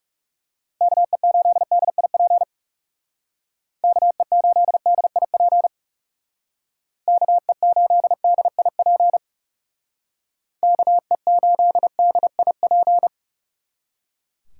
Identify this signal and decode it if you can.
Morse code, heard as a keyed tone, about 36 wpm, 700 Hz. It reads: KE8DIP KE8DIP KE8DIP KE8DIP
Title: Speed Racing - USA 2x3 Call Signs 30wpm